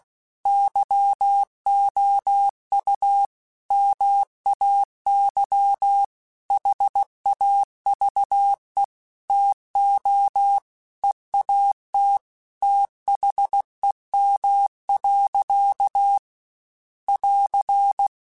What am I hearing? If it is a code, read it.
Morse code, heard as a keyed tone, about 16 wpm, 800 Hz. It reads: YOU MAY HAVE TO EAT THEM. <AR>
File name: Chuckles / Chuckles-mp3s / SoftWords @15